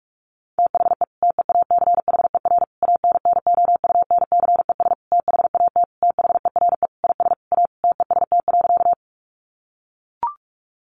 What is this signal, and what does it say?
Morse code, heard as a keyed tone, about 44 wpm, 700 Hz. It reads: THE TEACHER ANNOUNCES THAT THERE IS A TEST.
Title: Additional Easy Phrases 45wpm